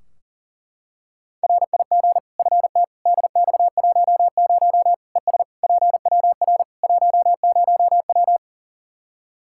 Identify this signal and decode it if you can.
Morse code, heard as a keyed tone, about 40 wpm, 700 Hz. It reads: RIG FT DX10 ES PWR 10W
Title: Practice QSO - Exchange 2 - Info - 40wpm